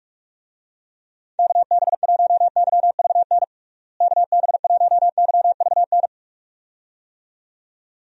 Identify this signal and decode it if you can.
Morse code, heard as a keyed tone, about 45 wpm, 700 Hz. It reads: KB1YVN KB1YVN
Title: USA Call Signs - 6 Characters - Repeated 2x - 45wpm